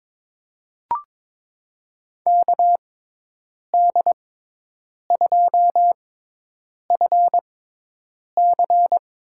Morse code, heard as a keyed tone, about 23 wpm, spaced as for 8 wpm, 700 Hz. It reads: KD2FC
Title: USA Call Signs - 6x Character Spacing - 22wpm